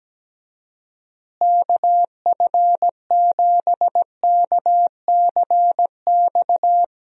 Morse code, heard as a keyed tone, about 17 wpm, 700 Hz. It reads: KF7KCX
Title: USA Call Signs - 6 Characters - Repeated 3x - 17wpm